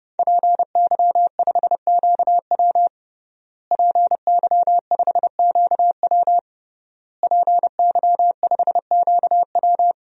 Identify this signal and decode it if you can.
Morse code, heard as a keyed tone, about 30 wpm, 700 Hz. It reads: PY5QW PY5QW PY5QW
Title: DXCC-100 Call Signs - 5 Characters - Repeated 3x - 30wpm